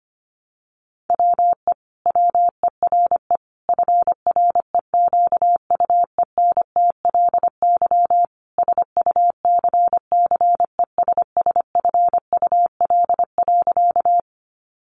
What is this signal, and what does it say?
Morse code, heard as a keyed tone, about 25 wpm, 700 Hz. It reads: WE WERE FREQUENTLY SUCCESSFUL.
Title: Sentences from Top 1300 Words 25wpm